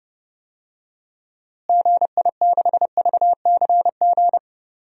Morse code, heard as a keyed tone, about 32 wpm, 700 Hz. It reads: GI6VCG
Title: DXCC-100 Call Signs - 6 Characters - Repeated 2x - 30wpm